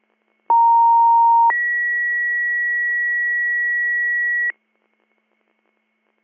Two-tone page, heard as a one-second tone, then a three-second tone, about 930 then 1950 Hz. It is heard from 0.5 to 4.5 s.